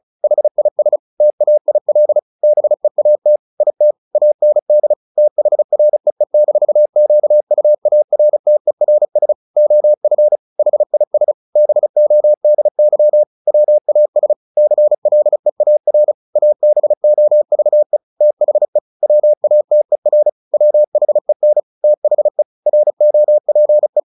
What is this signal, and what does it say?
Morse code, heard as a keyed tone, about 34 wpm, 600 Hz. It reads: HIS TAIL BEAT IT AND THREE-QUARTERS OF HIS BODY WAS CLEAR ABOVE THE WATER WHEN THE ROPE
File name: the old man and the sea 2 35wpm